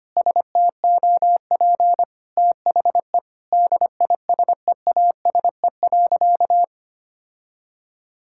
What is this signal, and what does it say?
Morse code, heard as a keyed tone, about 25 wpm, 700 Hz. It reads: STOP THE DISEASE.
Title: Sentences from Top 1000 Words - Review Part 6 - 25wpm